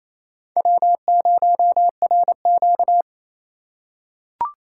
Morse code, heard as a keyed tone, about 27 wpm, 700 Hz. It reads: W0RQ
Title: USA Call Signs - 4 Characters - 4x Character Spacing - 28wpm